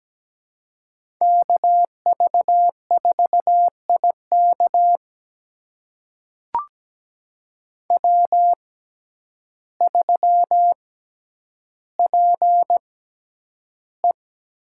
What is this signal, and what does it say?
Morse code, heard as a keyed tone, about 17 wpm, spaced as for 9 wpm, 700 Hz. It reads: KV4IK W3PE
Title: USA Call Signs - 5 Characters - 6x Character Spacing - 17wpm